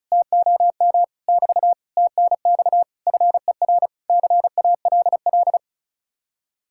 Morse code, heard as a keyed tone, about 48 wpm, spaced as for 42 wpm, 700 Hz. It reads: TOM <BT> TNX FER CALL
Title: Speed-Racing Common QSO Phrases Encore - 35wpm